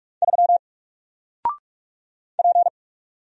Morse code, heard as a keyed tone, about 44 wpm, 700 Hz. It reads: 3 P
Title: Single Character (Letter:Number:Punctuation:Prosign) 45wpm